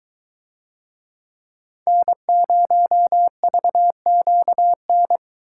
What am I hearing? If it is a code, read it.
Morse code, heard as a keyed tone, about 21 wpm, 700 Hz. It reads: N0VQN